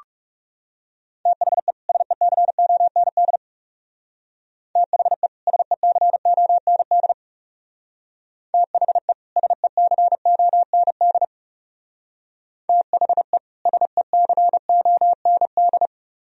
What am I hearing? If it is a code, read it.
Morse code, heard as a keyed tone, about 35 wpm, 700 Hz. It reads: THE SECOND THE SECOND THE SECOND THE SECOND
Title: Speed-Racing - Sets of 2 Words 30wpm